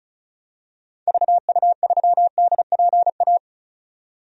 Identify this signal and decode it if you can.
Morse code, heard as a keyed tone, about 34 wpm, 700 Hz. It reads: VU3DPA
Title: DXCC-100 Call Signs - 6 Characters - 35wpm